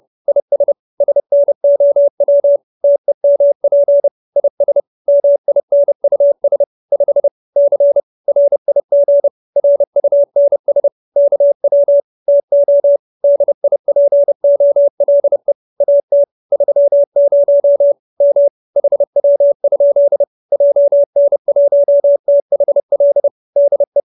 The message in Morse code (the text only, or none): IS SNOW TEMP IS MINUS 5 C RIG RUNS KW TO DIPOLE AT 30 M HW? JN1THL DE